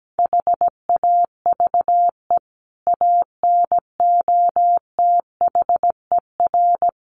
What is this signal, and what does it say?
Morse code, heard as a keyed tone, about 17 wpm, 700 Hz. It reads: HAVE ANOTHER
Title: Sentences from Top 1300 Words 17wpm